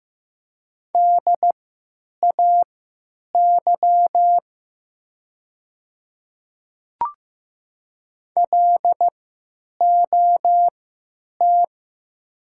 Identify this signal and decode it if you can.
Morse code, heard as a keyed tone, about 15 wpm, spaced as for 8 wpm, 700 Hz. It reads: DAY LOT
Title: 3x Character Spacing - Top 500 Words - 15wpm